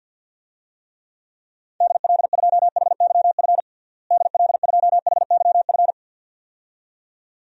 Morse code, heard as a keyed tone, about 53 wpm, 700 Hz. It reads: DL2HYF DL2HYF
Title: DXCC-100 Call Signs - 6 Characters - Repeated 2x - 50wpm